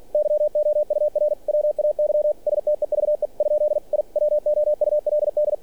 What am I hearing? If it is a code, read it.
Morse code, heard as a keyed tone, about 46 wpm, 600 Hz. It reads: YOUR WAY STEVE ? I WOULD
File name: archived / CRSnet